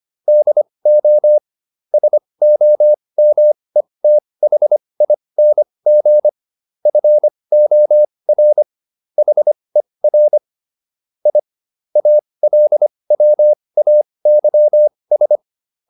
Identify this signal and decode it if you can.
Morse code, heard as a keyed tone, about 25 wpm, spaced as for 20 wpm, 600 Hz. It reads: DO SOMETHING FOR HER I ALWAYS